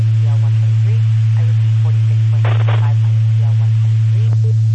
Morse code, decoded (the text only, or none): none